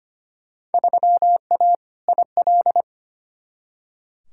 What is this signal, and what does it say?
Morse code, heard as a keyed tone, about 25 wpm, 700 Hz. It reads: 3A IL
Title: Field Day Exchanges - 25wpm